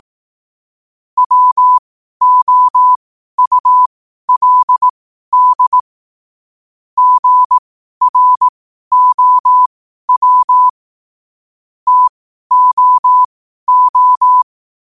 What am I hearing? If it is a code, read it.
Morse code, heard as a keyed tone, about 18 wpm, spaced as for 12 wpm, 1000 Hz. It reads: WOULD GROW TOO
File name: qotdfiles / 12wpm